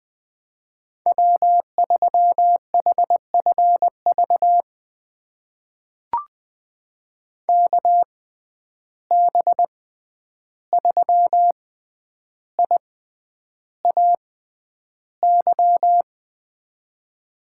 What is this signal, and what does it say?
Morse code, heard as a keyed tone, about 20 wpm, spaced as for 10 wpm, 700 Hz. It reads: W3HFV KB3IAY
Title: USA Call Signs - 6x Character Spacing - 20wpm